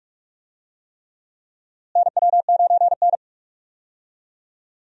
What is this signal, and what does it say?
Morse code, heard as a keyed tone, about 46 wpm, 700 Hz. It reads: NW9N
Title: USA Call Signs - 4 Characters - Repeated 3x - 45wpm